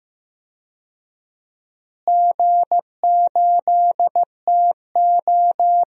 Morse code, heard as a keyed tone, about 15 wpm, 700 Hz. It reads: G8TO
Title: DXCC-100 Call Signs - 4 Characters - Repeated 3x - 15wpm